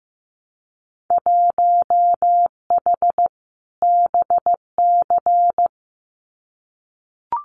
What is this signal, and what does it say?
Morse code, heard as a keyed tone, about 15 wpm, 700 Hz. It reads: 1H BC
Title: Winter Field Day Exchanges 15wpm